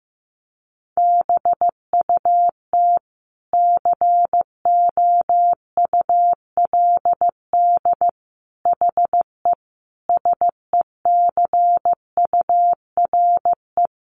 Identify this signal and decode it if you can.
Morse code, heard as a keyed tone, about 15 wpm, 700 Hz. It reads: BUT COULD HE SECURE
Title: Sentences from Top 1200 Words 15wpm